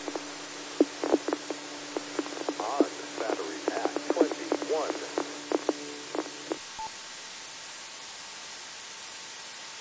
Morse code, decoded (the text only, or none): none